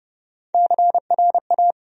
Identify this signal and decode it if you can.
Morse code, heard as a keyed tone, about 30 wpm, 700 Hz. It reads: CRA
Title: USA Call Sign Postfixes - 3 Letters 30wpm